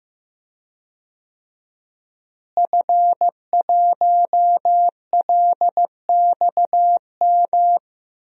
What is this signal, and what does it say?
Morse code, heard as a keyed tone, about 15 wpm, 700 Hz. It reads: F1LXM
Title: DXCC-100 Call Signs - 5 Characters - Repeated 2x - 15wpm